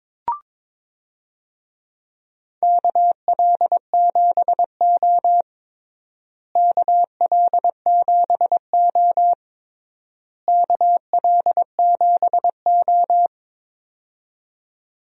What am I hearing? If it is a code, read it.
Morse code, heard as a keyed tone, about 22 wpm, 700 Hz. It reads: KL7O KL7O KL7O
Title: DXCC-100 Call Signs - 4 Characters - Repeated 3x - 22wpm